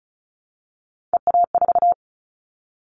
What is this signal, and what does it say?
Morse code, heard as a keyed tone, about 36 wpm, 700 Hz. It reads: EA4
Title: Callsign Prefixes for Top 25 DXCC - 35wpm